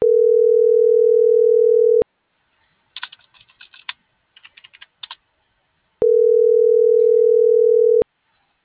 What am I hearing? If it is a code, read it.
Ringback tone (North America), 440 + 480 Hz.